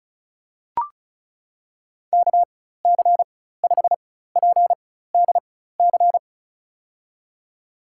Morse code, heard as a keyed tone, about 36 wpm, spaced as for 16 wpm, 700 Hz. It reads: KC5PDC
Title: USA Call Signs - 6 Characters - 4x Character Spacing - 35wpm